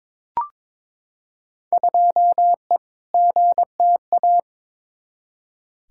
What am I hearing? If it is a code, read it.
Morse code, heard as a keyed tone, about 22 wpm, 700 Hz. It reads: 2E GTA
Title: Field Day Exchanges Encore - 22wpm